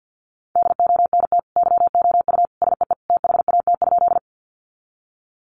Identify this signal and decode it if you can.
Morse code, heard as a keyed tone, about 50 wpm, 700 Hz. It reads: DONT YOU SEE THAT?
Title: Sentences from Top 300 Words 50wpm